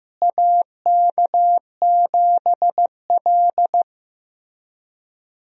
Morse code, heard as a keyed tone, about 15 wpm, 700 Hz. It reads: AK7L
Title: USA Call Signs - 4 Characters - Repeated 3x - 15wpm